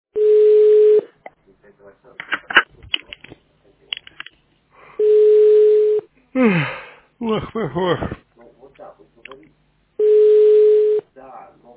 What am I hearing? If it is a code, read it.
Ringback tone (Europe, 425 Hz), with speech.